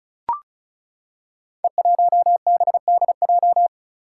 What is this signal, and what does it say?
Morse code, heard as a keyed tone, about 36 wpm, 700 Hz. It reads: E1BDJ
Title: Code Groups - 5 Letter:Numbers - 35wpm